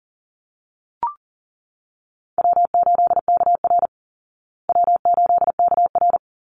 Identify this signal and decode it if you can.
Morse code, heard as a keyed tone, about 40 wpm, 700 Hz. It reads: W8KR W8KR
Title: Easy Call Signs Repeated 2x - 40wpm